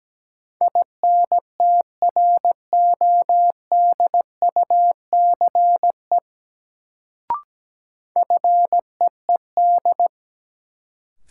Morse code, heard as a keyed tone, about 17 wpm, 700 Hz. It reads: INTRODUCE FEED
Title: Sentences from Top 1600 Words - New Word Drill Alt - 17wpm